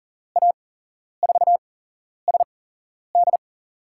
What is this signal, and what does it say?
Morse code, heard as a keyed tone, about 40 wpm, spaced as for 11 wpm, 700 Hz. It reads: A4SD
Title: USA Call Signs - 5 Characters - 8x Character Spacing - 40wpm